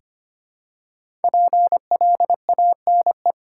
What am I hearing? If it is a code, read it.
Morse code, heard as a keyed tone, about 25 wpm, 700 Hz. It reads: PLANE
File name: 5 Letter Words - First 3 Letters Given - 25wpm